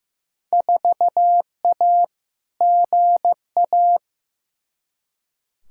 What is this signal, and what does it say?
Morse code, heard as a keyed tone, about 15 wpm, 700 Hz. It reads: 4A GA
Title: Field Day Exchanges - 15wpm